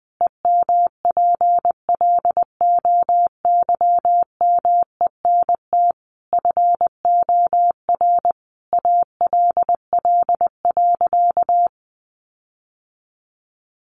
Morse code, heard as a keyed tone, about 20 wpm, 700 Hz. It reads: EMPLOYMENT FOR ALL.
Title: Sentences from Top 1500 Words 20wpm